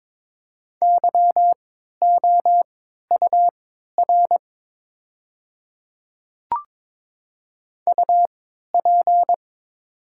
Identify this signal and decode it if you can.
Morse code, heard as a keyed tone, about 22 wpm, spaced as for 12 wpm, 700 Hz. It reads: YOUR UP
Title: 3x Character Spacing - Top 100 Words - 22wpm